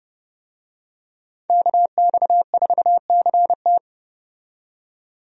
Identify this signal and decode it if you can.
Morse code, heard as a keyed tone, about 31 wpm, 700 Hz. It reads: KX4CT
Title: USA Call Signs - 5 Characters - 30wpm